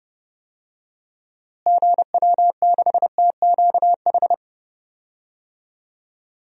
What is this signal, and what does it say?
Morse code, heard as a keyed tone, about 30 wpm, 700 Hz. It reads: GW6TQH
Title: DXCC-100 Call Signs - 6 Characters - Repeated 2x - 30wpm